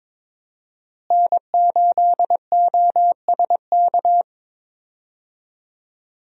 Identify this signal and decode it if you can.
Morse code, heard as a keyed tone, about 22 wpm, 700 Hz. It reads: N8OSK
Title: USA Call Signs - 5 Characters - Repeated 2x - 22wpm